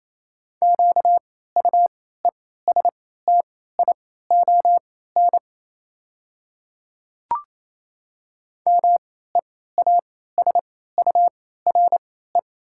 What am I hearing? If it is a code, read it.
Morse code, heard as a keyed tone, about 27 wpm, spaced as for 16 wpm, 700 Hz. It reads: QUESTION MEASURE
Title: 3x Character Spacing - Top 500 Words - 28wpm